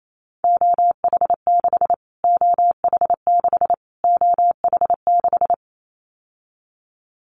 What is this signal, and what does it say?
Morse code, heard as a keyed tone, about 28 wpm, 700 Hz. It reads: OH6 OH6 OH6
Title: Callsign Prefixes for Top 25 DXCC - Repeated 3x - 28wpm